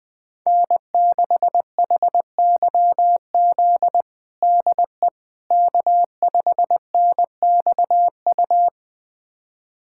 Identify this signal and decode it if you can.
Morse code, heard as a keyed tone, about 20 wpm, 700 Hz. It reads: N6HYZ DE K5NXU
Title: USA Callsign DE Callsign - 5 Characters - 20wpm